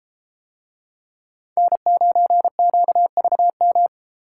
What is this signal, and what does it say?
Morse code, heard as a keyed tone, about 32 wpm, 700 Hz. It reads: N9QVM